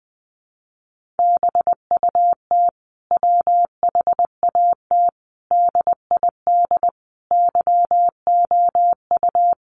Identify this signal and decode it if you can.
Morse code, heard as a keyed tone, about 20 wpm, 700 Hz. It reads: BUT WHAT DID YOU
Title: Sentences from Top 100 Words 20wpm